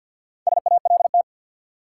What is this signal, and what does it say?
Morse code, heard as a keyed tone, about 52 wpm, 700 Hz. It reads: SALT
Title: Four Letter Words - 50wpm